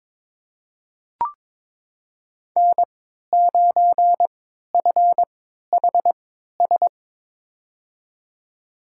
Morse code, heard as a keyed tone, about 21 wpm, spaced as for 12 wpm, 700 Hz. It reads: N9FHS
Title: USA Call Signs - 5 Characters - 3x Character Spacing - 22wpm